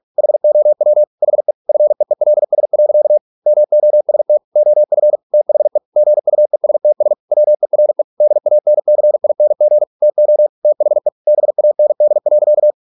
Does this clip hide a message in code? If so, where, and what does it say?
Yes: Morse code from 0.2 to 12.7 s, heard as a keyed tone, about 46 wpm, 600 Hz. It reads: HOW HE FEELS. MOST OF THE GUESTS WERE DANCING TO THE BAND.